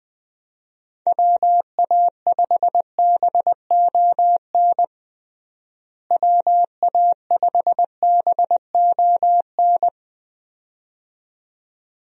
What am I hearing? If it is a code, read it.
Morse code, heard as a keyed tone, about 20 wpm, 700 Hz. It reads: WA5BON WA5BON
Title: USA Call Signs - 6 Characters - Repeated 2x - 20wpm